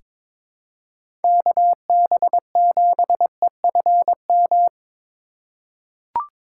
Morse code, heard as a keyed tone, about 22 wpm, 700 Hz. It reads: KB7EFM
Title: USA Call Signs - 6 Characters - 3x Character Spacing - 22wpm